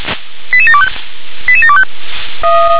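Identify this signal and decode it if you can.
Five-tone selective call (ZVEI), heard as quick runs of five tones, tones of about 70 ms. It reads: 88725 88725
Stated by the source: ZVEI five-tone call received by SDR, code 88713.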